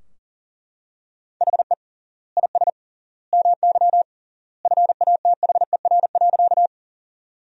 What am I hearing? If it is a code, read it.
Morse code, heard as a keyed tone, about 37 wpm, 700 Hz. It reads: HE IS MY FATHER.
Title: 3x Word-Spacing - Sentences from Top 300 Words - 40wpm